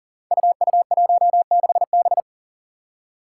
Morse code, heard as a keyed tone, about 39 wpm, 700 Hz. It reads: UU16B
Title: Five Challenging Letter-Number Combinations 40wpm